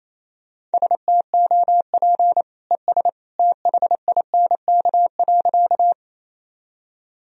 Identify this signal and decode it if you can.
Morse code, heard as a keyed tone, about 28 wpm, 700 Hz. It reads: STOP ES THINK.
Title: Sentences from Top 300 Words and Abbreviations 28wpm